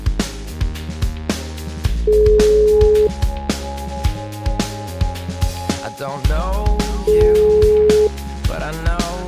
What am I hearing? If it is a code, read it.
Ringback tone (Europe, 425 Hz).